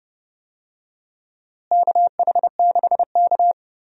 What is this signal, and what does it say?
Morse code, heard as a keyed tone, about 30 wpm, 700 Hz. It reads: KH6K
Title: DXCC-100 Call Signs - 4 Characters - Repeated 2x - 30wpm